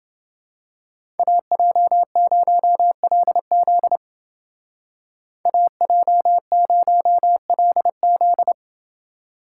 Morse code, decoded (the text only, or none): AJ0LZ AJ0LZ